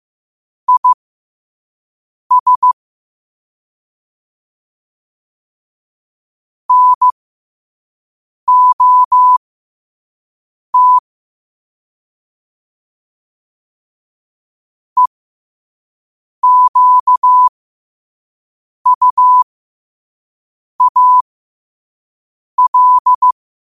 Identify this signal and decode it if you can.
Morse code, heard as a keyed tone, about 15 wpm, spaced as for 5 wpm, 1000 Hz. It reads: IS NOT EQUAL
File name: qotdfiles / 5wpm